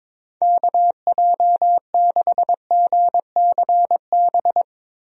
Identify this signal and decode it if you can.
Morse code, heard as a keyed tone, about 21 wpm, 700 Hz. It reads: KJ6GCB
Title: USA Call Signs - 6 Characters - Repeated 2x - 22wpm